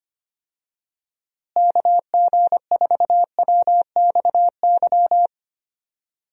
Morse code, heard as a keyed tone, about 25 wpm, 700 Hz. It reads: KG4WXY
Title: USA Call Signs - 6 Characters - Repeated 3x - 25wpm